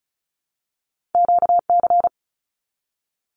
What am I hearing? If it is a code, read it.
Morse code, heard as a keyed tone, about 36 wpm, 700 Hz. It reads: QC
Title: Canadian Provinces and Territories 35wpm